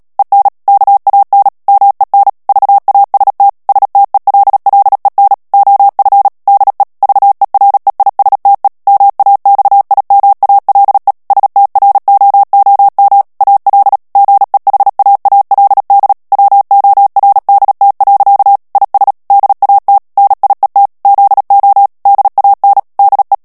Morse code, heard as a keyed tone, about 37 wpm, 800 Hz. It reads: EN KAN MEN VAST STELLEN OF DE VEREISTE MAXIMALE STROOM AL GEHAALD WORDT. IS DAT NIET ZO DAN DE
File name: morse mp3 / HOLLANDAIS texte clair 2 vitesse 35 mots minute